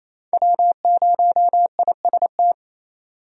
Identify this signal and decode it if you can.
Morse code, heard as a keyed tone, about 28 wpm, 700 Hz. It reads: W0IST
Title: USA Call Signs - 5 Characters - 28wpm